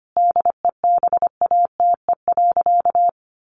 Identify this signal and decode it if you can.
Morse code, heard as a keyed tone, about 25 wpm, 700 Hz. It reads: DEBATE.
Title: Sentences from Top 1100 Words 25wpm